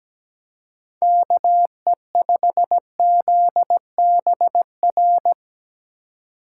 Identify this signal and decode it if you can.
Morse code, heard as a keyed tone, about 19 wpm, 700 Hz. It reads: KE5ZBR
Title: Speed Racing - USA 2x3 Call Signs 17wpm